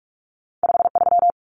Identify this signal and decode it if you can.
Morse code, heard as a keyed tone, about 46 wpm, 700 Hz. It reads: 53
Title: Common Cut Numbers - 2-Digit 45wpm